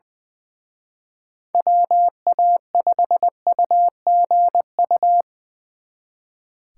Morse code, heard as a keyed tone, about 24 wpm, 700 Hz. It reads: WA5UGU
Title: Speed Racing - USA 2x3 Call Signs 20wpm